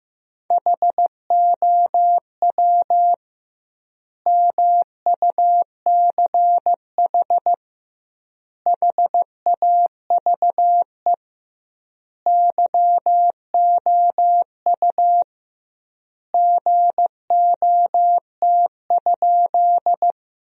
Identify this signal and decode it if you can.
Morse code, heard as a keyed tone, about 15 wpm, 700 Hz. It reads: HOW MUCH HAVE YOU GOT?
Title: 2x Word-Spacing - Sentences from Top 300 Words - 15wpm